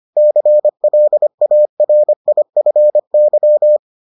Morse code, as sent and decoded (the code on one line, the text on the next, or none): -.-. .-.. .- .-. .. ..-. -.--
CLARIFY